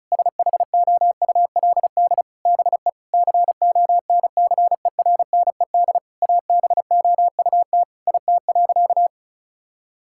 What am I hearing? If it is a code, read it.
Morse code, heard as a keyed tone, about 36 wpm, 700 Hz. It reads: SHOULD BE CONCERNED ABOUT IT.
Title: Sentences from Top 1000 Words - Review Part 6 - 35wpm